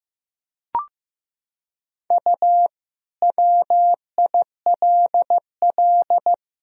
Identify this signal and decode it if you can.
Morse code, heard as a keyed tone, about 15 wpm, 700 Hz. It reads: U WILL
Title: Sentences from Top 300 Words and Abbreviations 15wpm